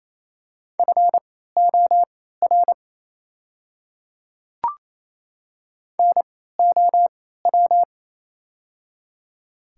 Morse code, heard as a keyed tone, about 28 wpm, spaced as for 16 wpm, 690 Hz. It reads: FOR NOW
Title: 3x Character Spacing - Top 100 Words - 28wpm